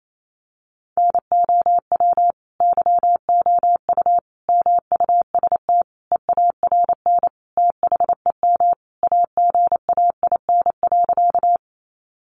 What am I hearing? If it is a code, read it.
Morse code, heard as a keyed tone, about 28 wpm, 700 Hz. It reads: NOW YOU MUST EARN THEM AGAIN.
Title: Sentences from Top 1500 Words 28wpm